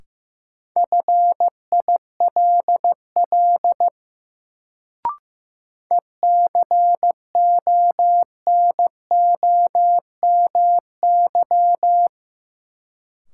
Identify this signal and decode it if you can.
Morse code, heard as a keyed tone, about 15 wpm, 700 Hz. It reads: FILL ECONOMY
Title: Sentences from Top 1000 Words - Review Part 3 - New Word Drill - 15wpm